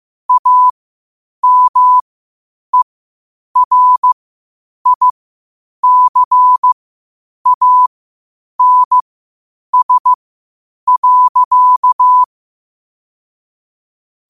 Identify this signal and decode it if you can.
Morse code, heard as a keyed tone, about 15 wpm, spaced as for 8 wpm, 1000 Hz. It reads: AMERICANS.
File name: qotdfiles / 8wpm